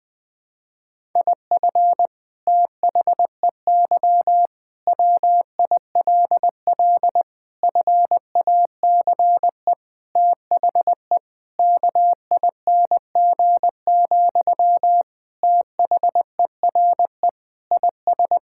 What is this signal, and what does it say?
Morse code, heard as a keyed tone, about 20 wpm, 700 Hz. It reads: IF THEY WILL FACE THE KING, THERE IS
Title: Sentences from Top 1000 Words - Review Part 6 - 20wpm